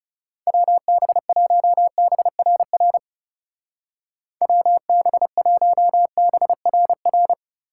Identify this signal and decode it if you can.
Morse code, heard as a keyed tone, about 38 wpm, 700 Hz. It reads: WB1BRR WB1BRR